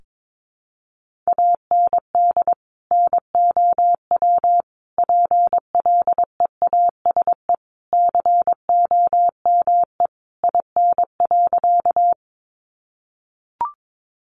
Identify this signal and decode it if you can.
Morse code, heard as a keyed tone, about 22 wpm, 700 Hz. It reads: AND NOW PLEASE COME IN.
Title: Sentences from Top 1500 Words 22wpm